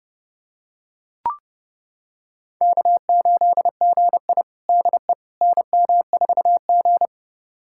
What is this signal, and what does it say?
Morse code, heard as a keyed tone, about 30 wpm, 700 Hz. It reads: K8GI DE NM4G
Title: USA Callsign DE Callsign - 4 Characters - 30wpm